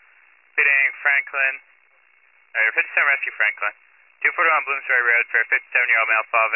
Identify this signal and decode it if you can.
Emergency Alert System (SAME) tones (header bursts), heard as harsh buzzing bursts, with speech. It is heard from 0.8 to 5.5 s.